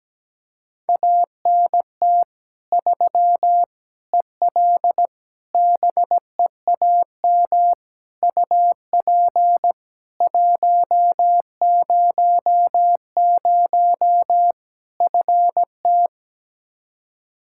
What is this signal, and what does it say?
Morse code, heard as a keyed tone, about 17 wpm, 700 Hz. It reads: ANT 3 EL BEAM UP 100 FT
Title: Practice QSO - Exchange 2 - Info - 17wpm